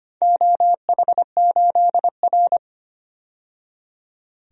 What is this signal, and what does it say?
Morse code, heard as a keyed tone, about 25 wpm, 700 Hz. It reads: OH8R
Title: DXCC-100 Call Signs - 4 Characters - 25wpm